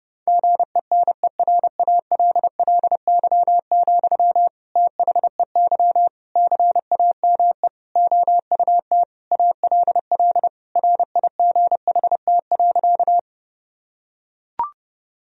Morse code, heard as a keyed tone, about 30 wpm, 700 Hz. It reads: GENERALLY, THEY CAME OUT ALL RIGHT.
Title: Sentences from Top 1000 Words - Review Part 4 - 30wpm